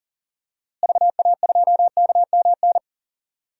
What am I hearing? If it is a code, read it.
Morse code, heard as a keyed tone, about 40 wpm, 700 Hz. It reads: VA2KMN
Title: DXCC-100 Call Signs - 6 Characters - Repeated 3x - 40wpm